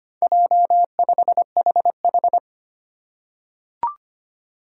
Morse code, heard as a keyed tone, about 25 wpm, 700 Hz. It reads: J5HH
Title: Four Challenging Letter-Number Combinations 25wpm